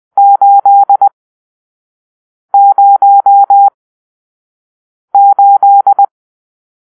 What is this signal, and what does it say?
Morse code, heard as a keyed tone, about 20 wpm, spaced as for 5 wpm, 800 Hz. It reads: 808